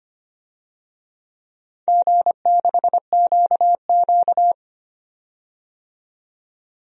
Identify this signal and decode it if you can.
Morse code, heard as a keyed tone, about 25 wpm, 700 Hz. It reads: G6QQ
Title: DXCC-100 Call Signs - 4 Characters - 25wpm